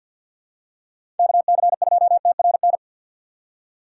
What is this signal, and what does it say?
Morse code, heard as a keyed tone, about 52 wpm, 700 Hz. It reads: KC2TRN
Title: USA Call Signs - 6 Characters - 4x Character Spacing - 50wpm